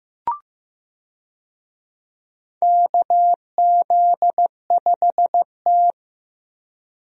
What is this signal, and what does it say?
Morse code, heard as a keyed tone, about 15 wpm, 700 Hz. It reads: KZ5T
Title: USA Call Signs - 4 Characters - Repeated 2x - 15wpm